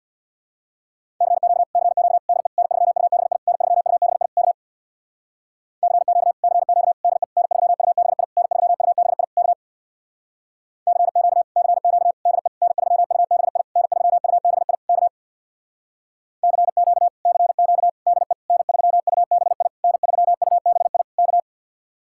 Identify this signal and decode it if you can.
Morse code, heard as a keyed tone, about 61 wpm, 700 Hz. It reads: CQ CQ DE N3UBI N3UBI K CQ CQ DE N3UBI N3UBI K CQ CQ DE N3UBI N3UBI K CQ CQ DE N3UBI N3UBI K
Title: Speed-Racing Common QSO Phrases - 50wpm